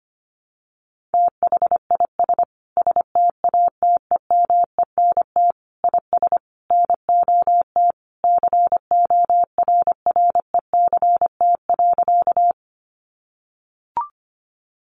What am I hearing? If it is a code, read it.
Morse code, heard as a keyed tone, about 25 wpm, 700 Hz. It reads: THIS STATEMENT IS NOT CORRECT.
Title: Sentences from Top 1200 Words 25wpm